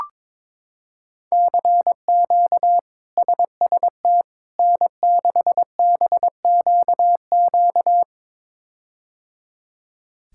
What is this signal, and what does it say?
Morse code, heard as a keyed tone, about 22 wpm, 700 Hz. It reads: CQ SST N6BQQ
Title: K1USN SST QSO Practice - 22wpm